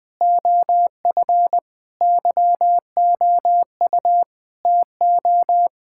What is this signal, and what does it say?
Morse code, heard as a keyed tone, about 21 wpm, 700 Hz. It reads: OF YOU TO
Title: Sentences from Top 1600 Words 20wpm